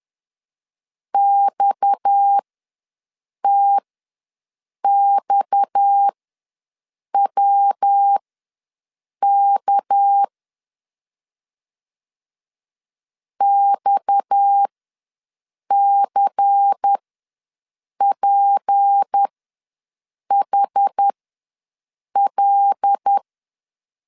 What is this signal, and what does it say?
Morse code, heard as a keyed tone, about 10 wpm, spaced as for 6 wpm, 800 Hz. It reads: XTXWK XCPHL
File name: LEZIONI / 6 PKX0 / 6 PKX ws 10wpm 3706